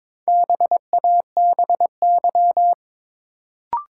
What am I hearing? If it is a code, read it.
Morse code, heard as a keyed tone, about 22 wpm, 700 Hz. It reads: BABY
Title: Sentences from Top 1300 Words - New Word Drill Alt - 22wpm